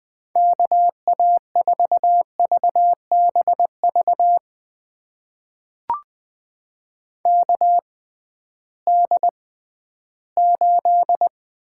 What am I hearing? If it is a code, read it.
Morse code, heard as a keyed tone, about 21 wpm, spaced as for 13 wpm, 700 Hz. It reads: KA4VBV KD8
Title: USA Call Signs - 6 Characters - 6x Character Spacing - 20wpm